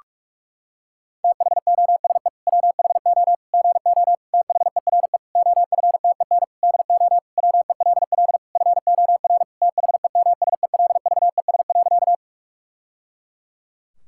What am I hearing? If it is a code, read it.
Morse code, heard as a keyed tone, about 46 wpm, 700 Hz. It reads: THOSE WHO GO THERE OFTEN DO WELL FOR THEMSELVES.
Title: Sentences from Top 1000 Words - Review Part 5 - 45wpm